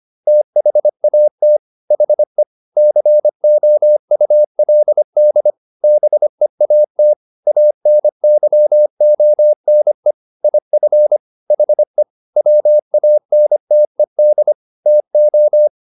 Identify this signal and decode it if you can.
Morse code, heard as a keyed tone, about 25 wpm, 600 Hz. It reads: THAT HE COULD BEAT ANYONE IF HE WANTED TO